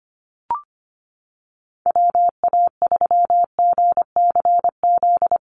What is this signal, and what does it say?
Morse code, heard as a keyed tone, about 25 wpm, 700 Hz. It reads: WA3GCZ
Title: USA Call Signs Repeated 3x 25wpm